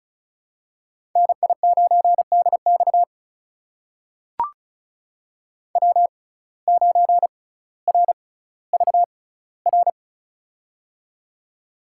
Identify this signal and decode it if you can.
Morse code, heard as a keyed tone, about 36 wpm, spaced as for 18 wpm, 700 Hz. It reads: NI9DX W9RVR
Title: USA Call Signs - 5 Characters - 6x Character Spacing - 35wpm